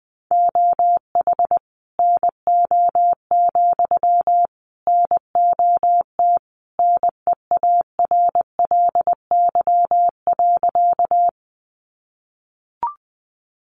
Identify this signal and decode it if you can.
Morse code, heard as a keyed tone, about 20 wpm, 700 Hz. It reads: OH NO, NOT NEARLY.
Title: Sentences from Top 1300 Words 20wpm